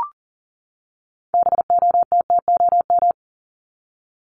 Morse code, heard as a keyed tone, about 40 wpm, 700 Hz. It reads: BOTTOM
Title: Top 751-1000 Words 40wpm